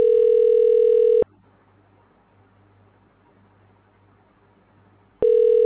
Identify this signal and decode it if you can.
Ringback tone (North America), 440 + 480 Hz.